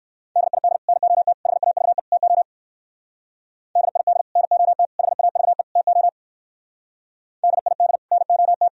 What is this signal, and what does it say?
Morse code, heard as a keyed tone, about 55 wpm, 700 Hz. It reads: DID NOT HAVE TO DID NOT HAVE TO DID NOT
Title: Speed-Racing - Sets of 4 Words 45wpm